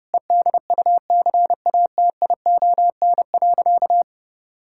Morse code, heard as a keyed tone, about 30 wpm, 700 Hz. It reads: EDUCATION.
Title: Sentences from Top 1000 Words - Review Part 6 - 30wpm